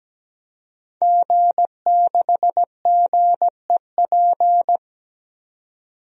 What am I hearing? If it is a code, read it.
Morse code, heard as a keyed tone, about 17 wpm, 700 Hz. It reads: G6GEP
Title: DXCC-100 Call Signs - 5 Characters - 17wpm